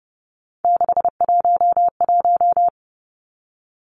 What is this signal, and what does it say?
Morse code, heard as a keyed tone, about 30 wpm, 700 Hz. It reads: 611
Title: Numbers - 3-Digit 30wpm